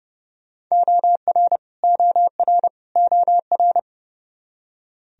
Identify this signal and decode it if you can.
Morse code, heard as a keyed tone, about 30 wpm, 700 Hz. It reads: OR OR OR
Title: Top 100 Words Repeated 30wpm